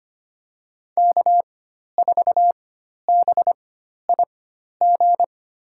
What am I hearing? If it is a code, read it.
Morse code, heard as a keyed tone, about 25 wpm, spaced as for 12 wpm, 700 Hz. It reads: K4BIG
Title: USA Call Signs - 5 Characters - 4x Character Spacing - 25wpm